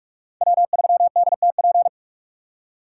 Morse code, heard as a keyed tone, about 47 wpm, 700 Hz. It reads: W3DTP
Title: USA Call Signs - 5 Characters - 4x Character Spacing - 45wpm